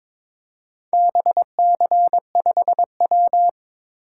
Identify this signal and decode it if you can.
Morse code, heard as a keyed tone, about 22 wpm, 700 Hz. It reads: BC5W
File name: practice / 22wpm / Four Letter-Number Combinations - 22wpm